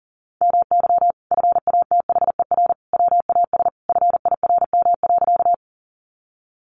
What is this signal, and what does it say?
Morse code, heard as a keyed tone, about 40 wpm, 700 Hz. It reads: MY FATHER WAS FIRM.
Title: Sentences from Top 1400 Words 40wpm